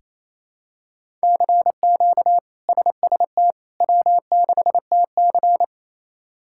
Morse code, heard as a keyed tone, about 27 wpm, 700 Hz. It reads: CQ SST W6TC
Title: K1USN SST QSO Practice - 28wpm